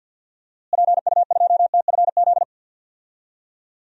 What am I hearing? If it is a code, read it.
Morse code, heard as a keyed tone, about 52 wpm, 700 Hz. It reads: PU2TFZ
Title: DXCC-100 Call Signs - 6 Characters - Repeated 2x - 50wpm